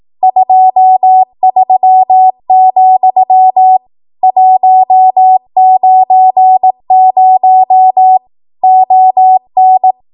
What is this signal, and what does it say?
Morse code, heard as a keyed tone, about 18 wpm, 750 Hz. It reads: 23, 190 ON